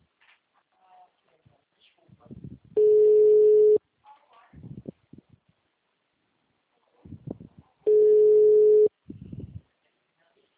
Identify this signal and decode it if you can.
Ringback tone (Europe, 425 Hz).
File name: out- -2008- - - .72511